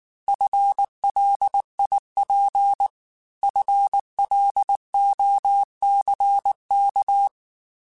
Morse code, heard as a keyed tone, about 19 wpm, 800 Hz. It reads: FLIP FLOCK